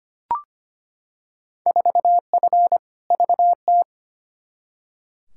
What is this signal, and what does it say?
Morse code, heard as a keyed tone, about 25 wpm, 700 Hz. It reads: 4F VT
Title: Field Day Exchanges Encore - 25wpm